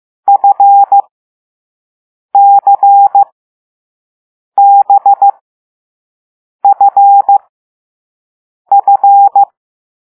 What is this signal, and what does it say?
Morse code, heard as a keyed tone, about 15 wpm, spaced as for 5 wpm, 800 Hz. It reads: FCBFF
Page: Play Lesson 6 C - all letters in previous lessons